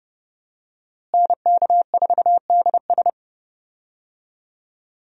Morse code, heard as a keyed tone, about 31 wpm, 700 Hz. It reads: NK4DS
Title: USA Call Signs - 5 Characters - Repeated 2x - 30wpm